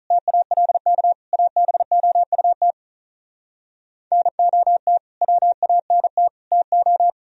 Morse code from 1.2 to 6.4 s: ABOUT NOT WANT

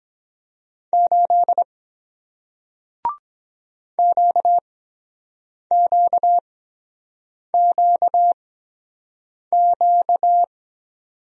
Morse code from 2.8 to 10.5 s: QQQQ

